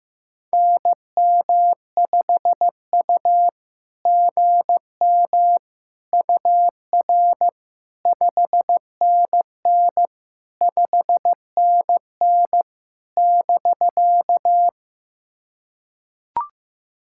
Morse code, heard as a keyed tone, about 15 wpm, 700 Hz. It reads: NM5U GM UR 5NN 5NN <BK>
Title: SOTA QSO Practice 15wpm